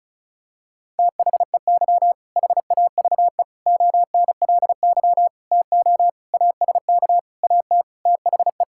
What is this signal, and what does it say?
Morse code, heard as a keyed tone, about 36 wpm, 700 Hz. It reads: THEY HAVE ONLY TO ASK AT THE
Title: Sentences from Top 1000 Words - Review Part 5 - 35wpm